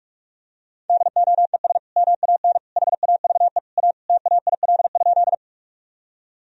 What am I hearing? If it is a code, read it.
Morse code, heard as a keyed tone, about 46 wpm, 700 Hz. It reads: DOES MAN HAVE A TAIL?
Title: Sentences from Top 1000 Words - Review Part 3 - 45wpm